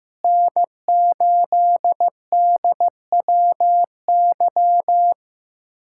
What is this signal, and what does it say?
Morse code, heard as a keyed tone, about 16 wpm, 700 Hz. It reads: N8DWY
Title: USA Callsign DE Callsign - 5 Characters - 15wpm